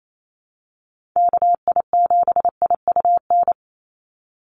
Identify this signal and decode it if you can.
Morse code, heard as a keyed tone, about 28 wpm, 700 Hz. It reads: KI7IUN
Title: USA Call Signs 28wpm